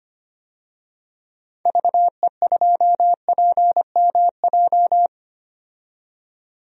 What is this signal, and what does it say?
Morse code, heard as a keyed tone, about 25 wpm, 700 Hz. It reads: VE2PMJ
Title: DXCC-100 Call Signs - 6 Characters - 25wpm